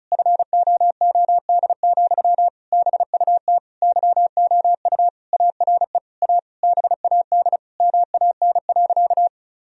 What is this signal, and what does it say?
Morse code, heard as a keyed tone, about 34 wpm, 700 Hz. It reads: FOOD, BUT YOU ARE A BAD MAN.
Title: Sentences from Top 1000 Words - Review Part 1 - 35wpm